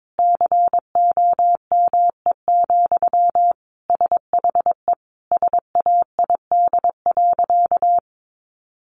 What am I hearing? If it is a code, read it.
Morse code, heard as a keyed tone, about 22 wpm, 700 Hz. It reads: COME, SHE SAID.
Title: Sentences from Top 100 Words 22wpm